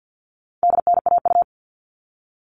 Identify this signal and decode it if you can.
Morse code, heard as a keyed tone, about 50 wpm, 700 Hz. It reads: DNAU
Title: All Cut Numbers - 4-Digit 50wpm